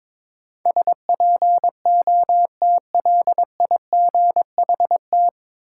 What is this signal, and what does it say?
Morse code, heard as a keyed tone, about 22 wpm, 700 Hz. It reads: SPOTLIGHT
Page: Morse Code Ninja practice 22wpm